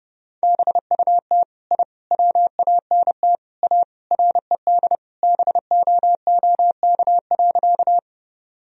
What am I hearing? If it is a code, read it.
Morse code, heard as a keyed tone, about 30 wpm, 700 Hz. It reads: BUT I WANT A RED BOOK.
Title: Sentences from Top 1000 Words - Review Part 2 - 30wpm